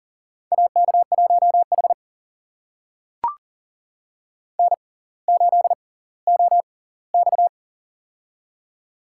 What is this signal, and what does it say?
Morse code, heard as a keyed tone, about 40 wpm, spaced as for 20 wpm, 700 Hz. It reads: AK1H N8OX
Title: USA Call Signs - 4 Characters - 6x Character Spacing - 40wpm